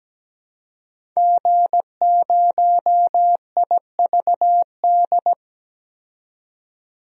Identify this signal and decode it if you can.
Morse code, heard as a keyed tone, about 17 wpm, 700 Hz. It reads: G0IVD
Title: DXCC-100 Call Signs - 5 Characters - 17wpm